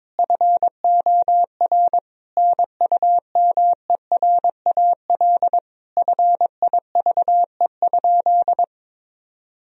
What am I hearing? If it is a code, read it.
Morse code, heard as a keyed tone, about 22 wpm, 700 Hz. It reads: FOR NUMERAL FIVE?
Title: Sentences from Top 1000 Words - Review Part 2 - 22wpm